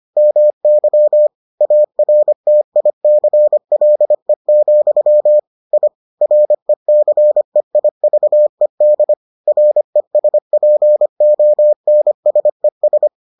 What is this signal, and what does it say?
Morse code, heard as a keyed tone, about 25 wpm, 600 Hz. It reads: MY ARTICLE, I RECEIVED RESPONSES